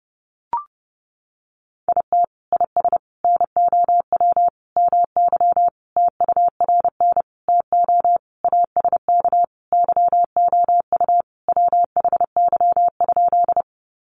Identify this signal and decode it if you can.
Morse code, heard as a keyed tone, about 30 wpm, 700 Hz. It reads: IT IS NOW MY TURN TO ASK YOU WHY?
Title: Sentences from Top 200 Words 30wpm